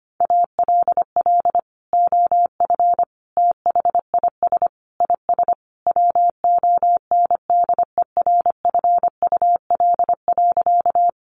Morse code, heard as a keyed tone, about 25 wpm, 700 Hz. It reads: ALL OF THIS IS WONDERFUL.
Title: Sentences from Top 1100 Words 25wpm